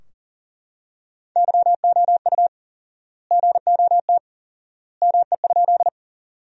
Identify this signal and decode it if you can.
Morse code, heard as a keyed tone, about 40 wpm, 700 Hz. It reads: YOU GOT ME?
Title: 4x Word-Spacing - Sentences from Top 300 Words - 40wpm